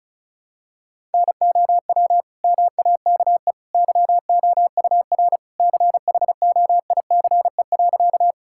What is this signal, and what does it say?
Morse code, heard as a keyed tone, about 36 wpm, 700 Hz. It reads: NOW MAKE YOUR CHOICE.
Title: Sentences from Top 1000 Words - Review Part 6 - 35wpm